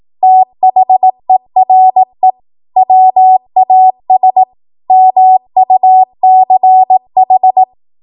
Morse code, heard as a keyed tone, about 18 wpm, 750 Hz. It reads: THERE WAS MUCH
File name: Archive / 18 WPM / 18WPM